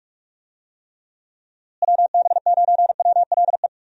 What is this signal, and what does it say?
Morse code, heard as a keyed tone, about 46 wpm, 700 Hz. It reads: WB9WLE
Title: USA Call Signs - 6 Characters - Repeated 3x - 45wpm